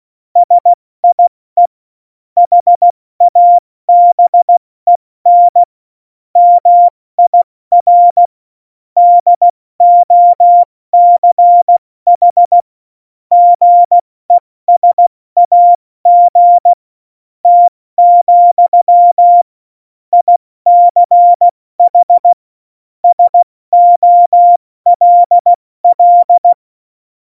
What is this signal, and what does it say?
Morse code, heard as a keyed tone, about 16 wpm, spaced as for 14 wpm, 700 Hz. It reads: SIE HABEN MIR DOCH GESAGT, ICH SOLL
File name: Morsekurs / Witze / Witz 17 April 2018, 70 F80